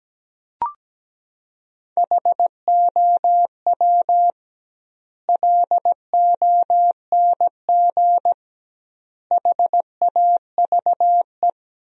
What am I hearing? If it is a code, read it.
Morse code, heard as a keyed tone, about 17 wpm, 700 Hz. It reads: HOW LONG HAVE
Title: 2x Word-Spacing - Sentences from Top 200 Words - 17wpm